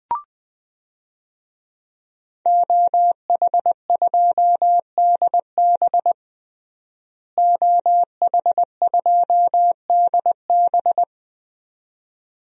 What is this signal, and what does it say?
Morse code, heard as a keyed tone, about 20 wpm, 700 Hz. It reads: OH2DB OH2DB
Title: DXCC-100 Call Signs - 5 Characters - Repeated 2x - 20wpm